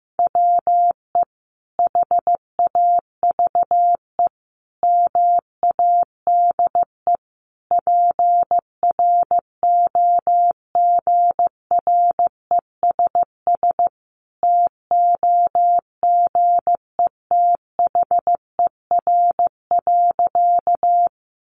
Morse code, heard as a keyed tone, about 15 wpm, 700 Hz. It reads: WE HAVE MADE PROGRESS TOGETHER.
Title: Sentences from Top 1400 Words 15wpm